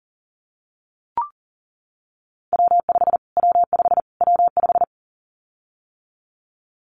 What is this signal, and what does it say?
Morse code, heard as a keyed tone, about 40 wpm, 700 Hz. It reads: W5 W5 W5
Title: Callsign Prefixes for Top 25 DXCC - Repeated 3x - 40wpm